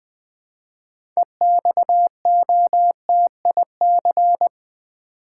Morse code, heard as a keyed tone, about 20 wpm, 700 Hz. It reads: EXOTIC
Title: 6 Letter Words - First 3 Letters Given - 20wpm